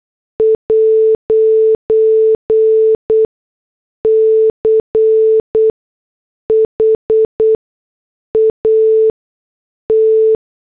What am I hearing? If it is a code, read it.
Morse code, heard as a keyed tone, about 8 wpm, spaced as for 6 wpm, 440 Hz. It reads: 'CHAT